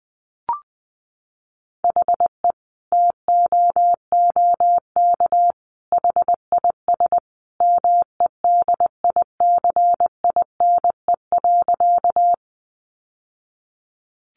Morse code, heard as a keyed tone, about 20 wpm, 700 Hz. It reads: HE TOOK HIS MEDICINE.
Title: Sentences from Top 900 Words 20wpm